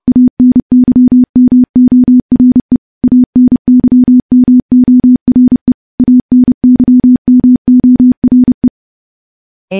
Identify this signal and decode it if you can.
Morse code, heard as a keyed tone, about 32 wpm, 250 Hz. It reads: ANYMORE ANYMORE ANYMORE
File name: morse / morse-training-30WPM